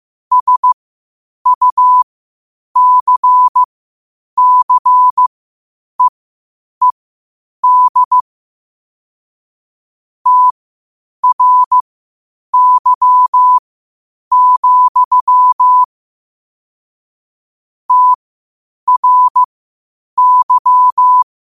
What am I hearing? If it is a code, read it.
Morse code, heard as a keyed tone, about 15 wpm, spaced as for 8 wpm, 1000 Hz. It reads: SUCCEED TRY, TRY
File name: qotdfiles / 8wpm